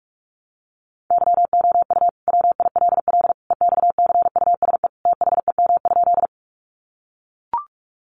Morse code, heard as a keyed tone, about 44 wpm, 700 Hz. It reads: YOU WILL EXCUSE THEM?
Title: Sentences from Top 700 Words 45wpm